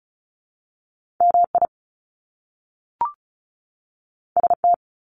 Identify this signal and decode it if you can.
Morse code, heard as a keyed tone, about 36 wpm, 700 Hz. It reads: MI ST
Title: Top Two Letter Combinations in Words 35wpm